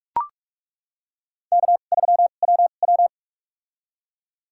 Morse code, heard as a keyed tone, about 42 wpm, spaced as for 31 wpm, 700 Hz. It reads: K3WW
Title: USA Call Signs - 2x Character Spacing - 45wpm